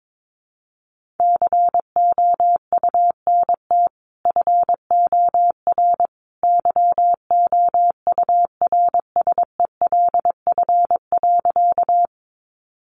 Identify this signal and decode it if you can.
Morse code, heard as a keyed tone, about 22 wpm, 700 Hz. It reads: COUNT FOR YOURSELF.
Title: Sentences from Top 1500 Words 22wpm